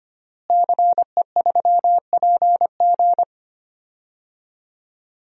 Morse code, heard as a keyed tone, about 25 wpm, 700 Hz. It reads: CE3PG
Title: DXCC-100 Call Signs - 5 Characters - Repeated 2x - 25wpm